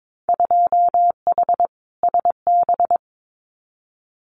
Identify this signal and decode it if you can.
Morse code, heard as a keyed tone, about 22 wpm, 700 Hz. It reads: 2H SB
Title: Winter Field Day Exchanges Encore 22wpm